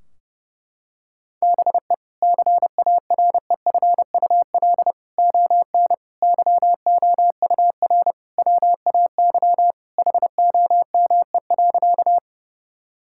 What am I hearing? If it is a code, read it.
Morse code, heard as a keyed tone, about 28 wpm, 700 Hz. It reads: BE CAREFUL ON YOUR WAY HOME.
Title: Sentences from Top 1600 Words 30wpm